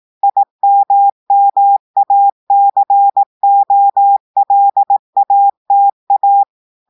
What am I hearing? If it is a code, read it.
Morse code, heard as a keyed tone, about 18 wpm, 800 Hz. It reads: IMMACOLATA
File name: audios / portugais / portugais texte en clair 1 vitesse 18 mots minutes